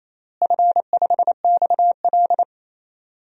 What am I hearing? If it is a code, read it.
Morse code, heard as a keyed tone, about 28 wpm, 700 Hz. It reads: F5XL
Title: DXCC-100 Call Signs - 4 Characters - Repeated 3x - 28wpm